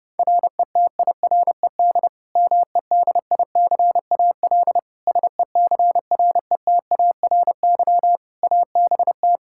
Morse code, heard as a keyed tone, about 30 wpm, 700 Hz. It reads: RETIRED MEDICAL SECRETARY ABT
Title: Practice QSO - Exchange 3 - More Info - 30wpm